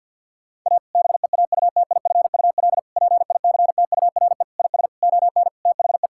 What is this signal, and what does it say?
Morse code, heard as a keyed tone, about 50 wpm, 700 Hz. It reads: A BEAUTIFUL PICTURE IS ON THE
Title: Sentences from Top 1000 Words - Review Part 6 - 50wpm